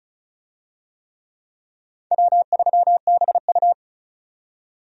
Morse code, heard as a keyed tone, about 34 wpm, 700 Hz. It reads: W3BU
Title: USA Call Signs - 4 Characters - 35wpm